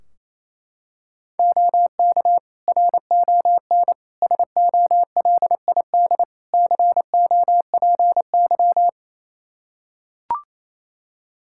Morse code, heard as a keyed tone, about 28 wpm, 700 Hz. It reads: OK RON SOLID COPY
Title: Practice QSO - Exchange 3 - More Info - 28wpm